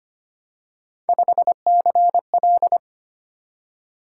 Morse code, heard as a keyed tone, about 25 wpm, 700 Hz. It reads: 5CL
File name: Three Letter-Number Combinations - 25wpm